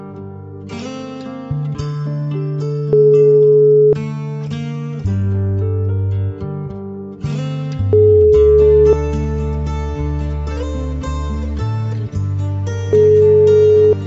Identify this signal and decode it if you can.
Ringback tone (Europe, 425 Hz).